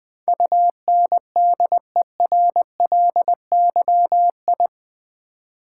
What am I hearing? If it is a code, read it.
Morse code, heard as a keyed tone, about 20 wpm, 700 Hz. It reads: UNDERLYI
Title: Build A Word - 4 Letters At A Time - 10-Letter Words - 20wpm